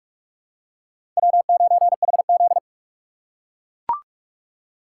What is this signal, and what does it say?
Morse code, heard as a keyed tone, about 43 wpm, 700 Hz. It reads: W9HZ
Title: USA Call Signs - 4 Characters - 3x Character Spacing - 45wpm